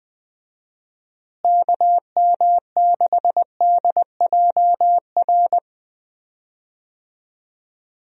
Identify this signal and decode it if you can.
Morse code, heard as a keyed tone, about 19 wpm, 700 Hz. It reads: KM6DJR
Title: USA Call Signs - 6 Characters - Repeated 3x - 20wpm